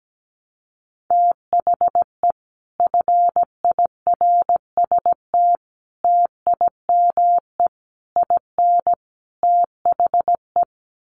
Morse code, heard as a keyed tone, about 17 wpm, 700 Hz. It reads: THE FIRST TIME IN THE
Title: Sets of 5 Words Encore 17wpm